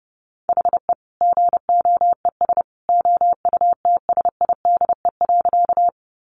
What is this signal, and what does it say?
Morse code, heard as a keyed tone, about 30 wpm, 700 Hz. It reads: HE GOES OUTSIDE.
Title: Easy Phrases 30wpm